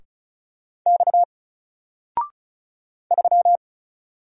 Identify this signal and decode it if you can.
Morse code, heard as a keyed tone, about 35 wpm, 700 Hz. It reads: X 3
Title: Single Letter-Number Alt 35wpm